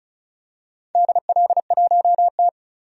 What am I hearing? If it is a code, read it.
Morse code, heard as a keyed tone, about 34 wpm, 700 Hz. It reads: DL1T
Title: DXCC-100 Call Signs - 4 Characters - 35wpm